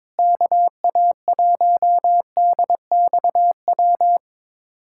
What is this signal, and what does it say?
Morse code, heard as a keyed tone, about 22 wpm, 700 Hz. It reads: KA1DXW
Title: USA Callsign DE Callsign - 6 Characters - 22wpm